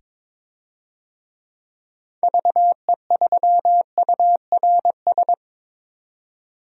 Morse code, heard as a keyed tone, about 22 wpm, 700 Hz. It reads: VE3URS
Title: DXCC-100 Call Signs - 6 Characters - 22wpm